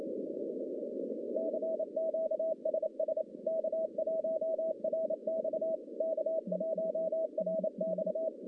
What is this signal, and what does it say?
Morse code, heard as a keyed tone, about 28 wpm, 640 Hz, with noise. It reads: CQ SS K1RX K1RX
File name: contests / CW-SS-k5tr- -1200